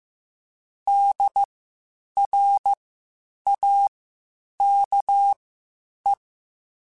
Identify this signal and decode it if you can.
Morse code, heard as a keyed tone, about 15 wpm, spaced as for 8 wpm, 800 Hz. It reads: DRAKE
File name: (spaced) / QSOs Spaced-mp3s / WD0FBT 1 @13 sp